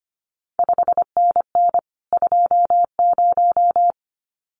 Morse code, heard as a keyed tone, about 25 wpm, 700 Hz. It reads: 5NN 20
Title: CQ WW DX Contest Exchange - 25wpm